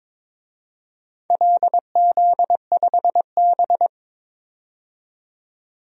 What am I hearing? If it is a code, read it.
Morse code, heard as a keyed tone, about 22 wpm, 700 Hz. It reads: LZ5B
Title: DXCC-100 Call Signs - 4 Characters - 22wpm